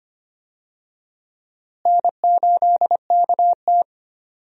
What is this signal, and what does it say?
Morse code, heard as a keyed tone, about 25 wpm, 700 Hz. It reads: N8KT
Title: USA Call Signs - 4 Characters - Repeated 2x - 25wpm